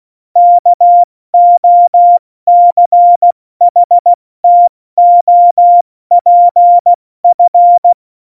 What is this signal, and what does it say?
Morse code, heard as a keyed tone, about 16 wpm, spaced as for 14 wpm, 700 Hz. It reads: KOCHTOPF
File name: Morsekurs / Witze / Witz 12. Januar 2021, 70 F80